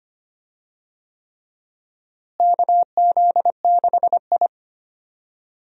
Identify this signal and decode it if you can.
Morse code, heard as a keyed tone, about 25 wpm, 700 Hz. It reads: KZ6I
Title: USA Call Signs - 4 Characters - Repeated 3x - 25wpm